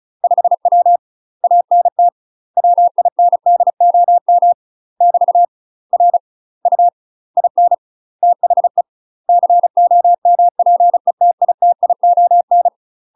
Morse code, heard as a keyed tone, about 34 wpm, 700 Hz. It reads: 5W ANT WINDOM <BT> R U IN THE COMPETITION
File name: UK MASTER QSOs TRAINING / 35WPM / UK MASTER QSOs 0035